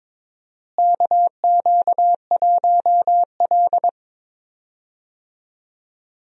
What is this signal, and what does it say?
Morse code, heard as a keyed tone, about 22 wpm, 700 Hz. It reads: KQ1L
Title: USA Call Signs - 4 Characters - Repeated 3x - 22wpm